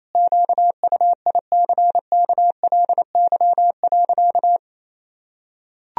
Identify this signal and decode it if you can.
Morse code, heard as a keyed tone, about 28 wpm, 700 Hz. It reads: QUICKLY.
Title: Sentences from Top 1000 Words - Review Part 1 - 28wpm